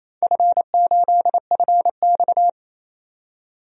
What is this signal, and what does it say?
Morse code, heard as a keyed tone, about 28 wpm, 700 Hz. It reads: F8FX
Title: DXCC-100 Call Signs - 4 Characters - 28wpm